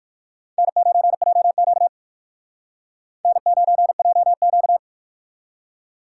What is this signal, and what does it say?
Morse code, heard as a keyed tone, about 59 wpm, 700 Hz. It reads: N9JQ N9JQ